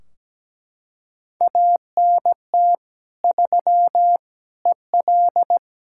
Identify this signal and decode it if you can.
Morse code, heard as a keyed tone, about 17 wpm, 700 Hz. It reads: ANT 3 EL
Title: Practice QSO - Exchange 2 - Info - 17wpm